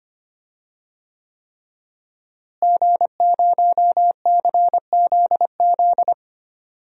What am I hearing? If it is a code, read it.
Morse code, heard as a keyed tone, about 25 wpm, 700 Hz. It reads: G0CZZ
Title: DXCC-100 Call Signs - 5 Characters - Repeated 3x - 25wpm